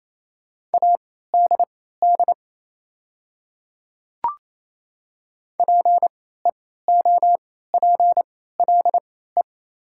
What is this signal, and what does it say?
Morse code, heard as a keyed tone, about 28 wpm, spaced as for 16 wpm, 700 Hz. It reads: ADD PEOPLE
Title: 3x Character Spacing - Top 200 Words - 28wpm